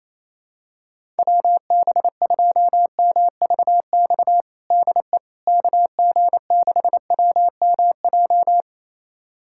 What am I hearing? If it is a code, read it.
Morse code, heard as a keyed tone, about 28 wpm, 700 Hz. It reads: WB2MVX DE KG6WMJ
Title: USA Callsign DE Callsign - 6 Characters - 28wpm